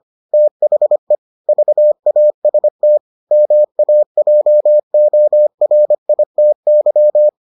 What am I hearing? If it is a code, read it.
Morse code, heard as a keyed tone, about 25 wpm, 600 Hz. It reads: THE VAST MAJORITY